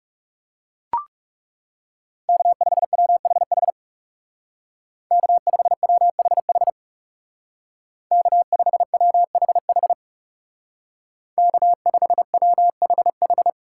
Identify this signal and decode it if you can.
Morse code, heard as a keyed tone, about 35 wpm, 700 Hz. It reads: K5WHH K5WHH K5WHH K5WHH
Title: Speed Racing - USA 1x3 Call Signs 30wpm